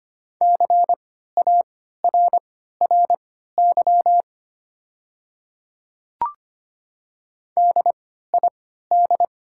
Morse code, heard as a keyed tone, about 25 wpm, spaced as for 14 wpm, 700 Hz. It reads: CARRY DID
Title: 3x Character Spacing - Top 500 Words - 25wpm